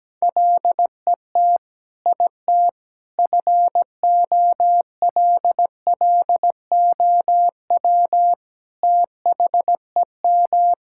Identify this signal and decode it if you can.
Morse code, heard as a keyed tone, about 17 wpm, 700 Hz. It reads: LET IT FOLLOW THEM
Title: Sentences from Top 1600 Words 17wpm